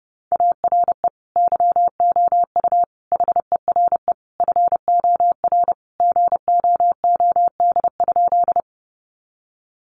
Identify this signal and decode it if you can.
Morse code, heard as a keyed tone, about 30 wpm, 700 Hz. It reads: ARE YOU HERE FOR GOOD?
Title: Sentences from Top 200 Words 30wpm